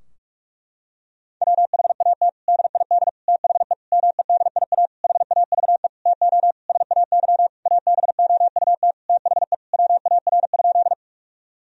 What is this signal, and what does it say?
Morse code, heard as a keyed tone, about 41 wpm, 700 Hz. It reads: WHAT DID THE MEDIA HAVE TO SAY ABOUT THE WAR?
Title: Sentences from Top 1000 Words - Review Part 4 - 45wpm